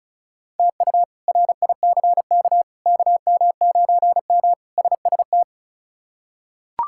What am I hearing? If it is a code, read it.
Morse code, heard as a keyed tone, about 36 wpm, 700 Hz. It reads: TU RICK KM9M SST
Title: K1USN SST QSO Practice - 35wpm